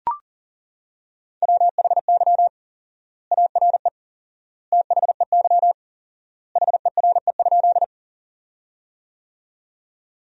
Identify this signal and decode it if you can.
Morse code, heard as a keyed tone, about 40 wpm, 700 Hz. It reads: WHY ARE THEY HERE?
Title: 4x Word-Spacing - Sentences from Top 200 Words - 40wpm